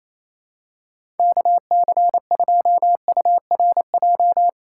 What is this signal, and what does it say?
Morse code, heard as a keyed tone, about 28 wpm, 700 Hz. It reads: KC2URJ
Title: USA Call Signs - 6 Characters - 4x Character Spacing - 28wpm